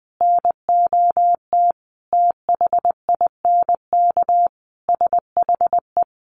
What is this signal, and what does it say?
Morse code, heard as a keyed tone, about 19 wpm, 700 Hz. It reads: NOT THINK SHE
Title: Sets of 5 Words 20wpm